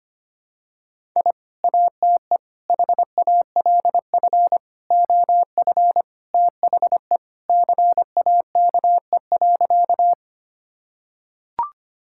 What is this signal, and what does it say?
Morse code, heard as a keyed tone, about 25 wpm, 700 Hz. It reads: I ATE HALF OF THE CAKE.
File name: practice / 25wpm / Sentences with 3-Word Phrases - 25wpm